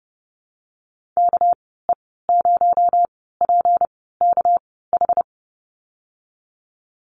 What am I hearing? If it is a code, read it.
Morse code, heard as a keyed tone, about 30 wpm, spaced as for 17 wpm, 700 Hz. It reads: KE0PKH
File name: Lesson 37g - Letter x - Call Signs 30wpm